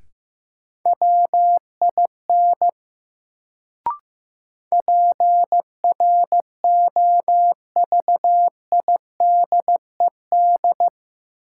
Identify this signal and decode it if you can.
Morse code, heard as a keyed tone, about 15 wpm, 700 Hz. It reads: WIN PROVIDED
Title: Sentences from Top 1500 Words - New Word Drill - 15wpm